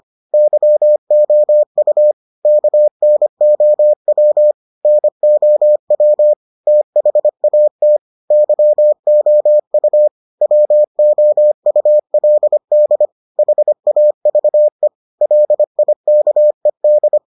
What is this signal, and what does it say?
Morse code, heard as a keyed tone, about 25 wpm, 600 Hz. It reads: YOU KNOW NOW THAT YOU WOULD HAVE LIKED